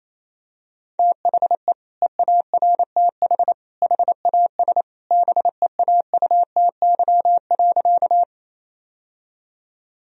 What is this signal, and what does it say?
Morse code, heard as a keyed tone, about 28 wpm, 700 Hz. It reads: THE EARTH HAS BEAUTY.
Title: Sentences from Top 1000 Words - Review Part 2 - 28wpm